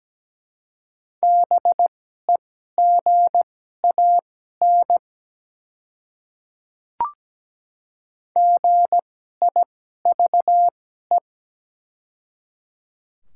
Morse code, heard as a keyed tone, about 17 wpm, spaced as for 12 wpm, 700 Hz. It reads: BEGAN GIVE
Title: 2x Character Spacing - Top 300 Words - 17wpm